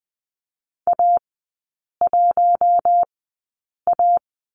Morse code, heard as a keyed tone, about 20 wpm, spaced as for 8 wpm, 700 Hz. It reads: A1A
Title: 053 Lesson 10e - Letter r - Three Characters 20wpm